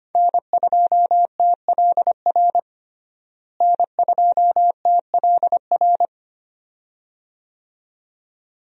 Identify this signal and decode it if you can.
Morse code, heard as a keyed tone, about 25 wpm, 700 Hz. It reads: N2TLR N2TLR
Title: USA Call Signs - 5 Characters - Repeated 2x - 25wpm